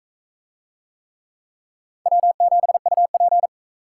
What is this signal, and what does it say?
Morse code, heard as a keyed tone, about 40 wpm, 700 Hz. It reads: W7UP